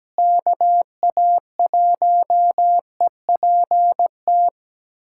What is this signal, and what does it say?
Morse code, heard as a keyed tone, about 17 wpm, 700 Hz. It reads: KA1EPT
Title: USA Callsign DE Callsign - 6 Characters - 17wpm